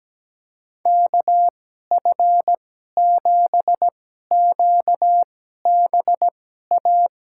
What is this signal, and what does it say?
Morse code, heard as a keyed tone, about 17 wpm, spaced as for 12 wpm, 700 Hz. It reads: KF7QBA
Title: USA Call Signs - 6 Characters - 2x Character Spacing - 17wpm